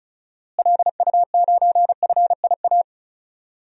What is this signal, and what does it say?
Morse code, heard as a keyed tone, about 34 wpm, 700 Hz. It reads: LU9FIA
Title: DXCC-100 Call Signs - 6 Characters - Repeated 3x - 35wpm